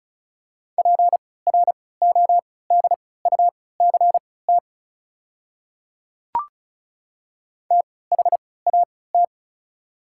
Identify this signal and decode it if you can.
Morse code, heard as a keyed tone, about 36 wpm, spaced as for 20 wpm, 690 Hz. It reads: PRODUCT THAT
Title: 3x Character Spacing - Top 500 Words - 35wpm